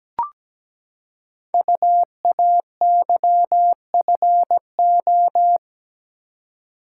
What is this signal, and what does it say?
Morse code, heard as a keyed tone, about 17 wpm, 700 Hz. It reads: UAYFO
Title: Code Groups - 5 Letters - 17wpm